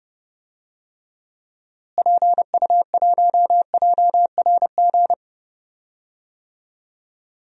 Morse code, heard as a keyed tone, about 30 wpm, 700 Hz. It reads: PU1JRG
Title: DXCC-100 Call Signs - 6 Characters - Repeated 2x - 30wpm